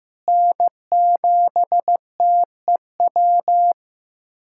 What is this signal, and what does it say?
Morse code, heard as a keyed tone, about 15 wpm, 700 Hz. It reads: N7TEW
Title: K1USN SST QSO Practice - 15wpm